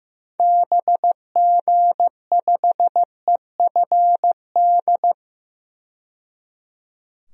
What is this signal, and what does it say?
Morse code, heard as a keyed tone, about 15 wpm, 700 Hz. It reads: BG5EFD
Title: DXCC-100 Call Signs - 6 Characters - 15wpm